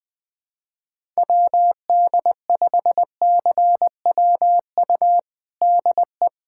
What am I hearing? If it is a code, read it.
Morse code, heard as a keyed tone, about 20 wpm, 700 Hz. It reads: WD5CWU DE
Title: USA Callsign DE Callsign 20wpm